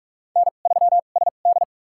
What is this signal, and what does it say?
Morse code, heard as a keyed tone, about 47 wpm, spaced as for 33 wpm, 700 Hz. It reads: N3SD
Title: USA Call Signs - 4 Characters - 2x Character Spacing - 45wpm